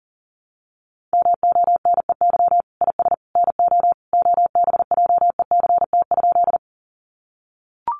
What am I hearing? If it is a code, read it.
Morse code, heard as a keyed tone, about 40 wpm, 700 Hz. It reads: MONEY IS NO OBJECT?
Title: Sentences from Top 500 Words 40wpm